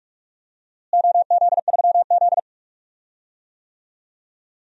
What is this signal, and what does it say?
Morse code, heard as a keyed tone, about 46 wpm, 700 Hz. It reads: OZ3Z
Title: DXCC-100 Call Signs - 4 Characters - Repeated 3x - 45wpm